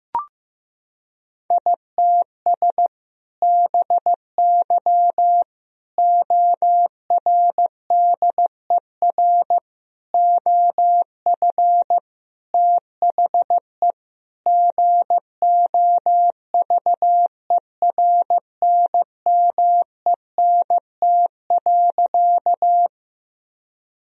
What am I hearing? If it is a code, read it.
Morse code, heard as a keyed tone, about 16 wpm, 700 Hz. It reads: ITS BY ORDER OF THE GOVERNMENT.
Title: Sentences from Top 1000 Words - Review Part 3 - 15wpm